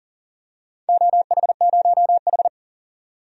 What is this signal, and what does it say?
Morse code, heard as a keyed tone, about 37 wpm, 700 Hz. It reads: OH0H
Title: DXCC-100 Call Signs - 4 Characters - 40wpm